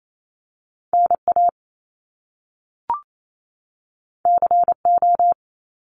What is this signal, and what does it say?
Morse code, heard as a keyed tone, about 28 wpm, 700 Hz. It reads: NA CO
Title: Top Two Letter Combinations in Words 28wpm